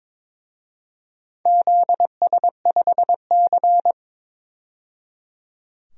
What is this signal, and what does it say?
Morse code, heard as a keyed tone, about 22 wpm, 700 Hz. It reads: ZS5C
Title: DXCC-100 Call Signs - 4 Characters - 22wpm